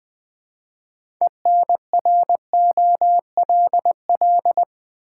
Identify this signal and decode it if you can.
Morse code, heard as a keyed tone, about 20 wpm, 700 Hz. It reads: ENROLL
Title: 6 Letter Words - First 3 Letters Given - 20wpm